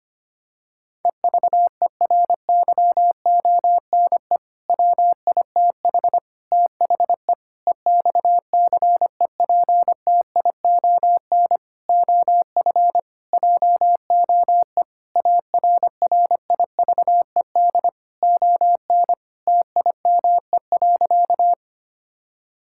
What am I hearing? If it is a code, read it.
Morse code, heard as a keyed tone, about 25 wpm, 700 Hz. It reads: EVERYONE WITH THE EXCEPTION OF JOE ARRIVED ON TIME.
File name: practice / 25wpm / Sentences with 4-Word Phrases - 25wpm